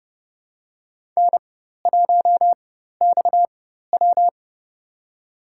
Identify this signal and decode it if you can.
Morse code, heard as a keyed tone, about 30 wpm, spaced as for 14 wpm, 700 Hz. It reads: N1XW
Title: USA Call Signs - 4 Characters - 4x Character Spacing - 30wpm